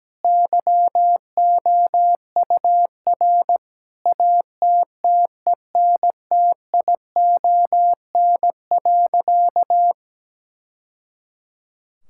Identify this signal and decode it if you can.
Morse code, heard as a keyed tone, about 17 wpm, 700 Hz. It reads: YOUR ATTENTION.
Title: Sentences from Top 1000 Words - Review Part 6 - 17wpm